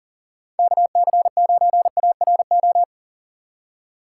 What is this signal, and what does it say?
Morse code, heard as a keyed tone, about 38 wpm, 700 Hz. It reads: KC9ARO
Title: USA Call Signs - 6 Characters - 3x Character Spacing - 40wpm